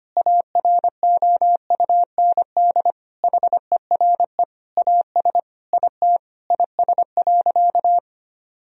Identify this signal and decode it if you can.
Morse code, heard as a keyed tone, about 25 wpm, 700 Hz. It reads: AROUND HERE AS IT IS.
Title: Sentences from Top 1000 Words - Review Part 3 - 25wpm